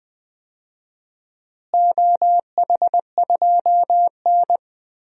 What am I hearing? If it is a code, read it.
Morse code, heard as a keyed tone, about 20 wpm, 700 Hz. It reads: OH2N
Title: DXCC-100 Call Signs - 4 Characters - Repeated 2x - 20wpm